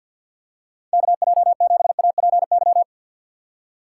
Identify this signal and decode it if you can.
Morse code, heard as a keyed tone, about 53 wpm, 700 Hz. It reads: KJ7APY
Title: USA Call Signs - 6 Characters - 6x Character Spacing - 50wpm